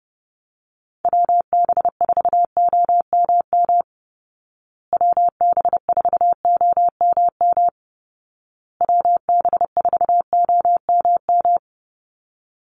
Morse code, heard as a keyed tone, about 30 wpm, 700 Hz. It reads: WB4OMM WB4OMM WB4OMM
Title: Call Signs Repeated 2x - 30wpm